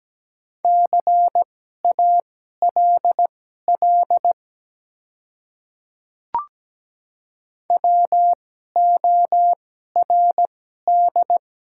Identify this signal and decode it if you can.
Morse code, heard as a keyed tone, about 17 wpm, spaced as for 12 wpm, 700 Hz. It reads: CALL WORD
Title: 2x Character Spacing - Top 100 Words - 17wpm